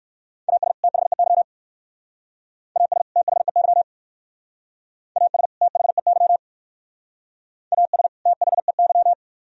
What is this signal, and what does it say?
Morse code, heard as a keyed tone, about 56 wpm, 700 Hz. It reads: AS THEY AS THEY AS THEY AS THEY